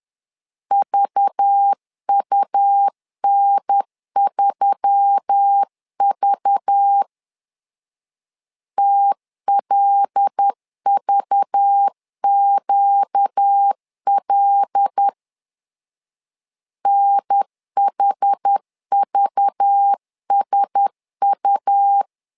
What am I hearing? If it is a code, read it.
Morse code, heard as a keyed tone, about 10 wpm, 800 Hz. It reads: VUN3V TLVQL NHVSU
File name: LEZIONI / 5 LBVC / 5 LBV 10wpm 2248